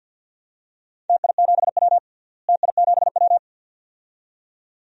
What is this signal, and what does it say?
Morse code, heard as a keyed tone, about 50 wpm, 700 Hz. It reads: TI7W TI7W
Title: DXCC-100 Call Signs - 4 Characters - Repeated 2x - 50wpm